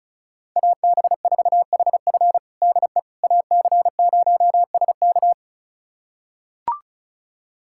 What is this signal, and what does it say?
Morse code, heard as a keyed tone, about 34 wpm, 700 Hz. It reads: AB4HF DE AC0SK
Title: USA Callsign DE Callsign - 5 Characters - 35wpm